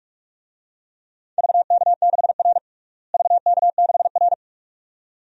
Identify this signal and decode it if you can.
Morse code, heard as a keyed tone, about 41 wpm, 700 Hz. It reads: VK6R VK6R
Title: DXCC-100 Call Signs - 4 Characters - Repeated 2x - 45wpm